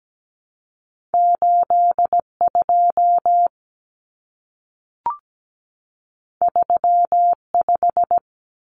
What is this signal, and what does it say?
Morse code, heard as a keyed tone, about 17 wpm, 700 Hz. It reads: 8 2 3 5
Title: Common Cut Numbers - 2-Digit 17wpm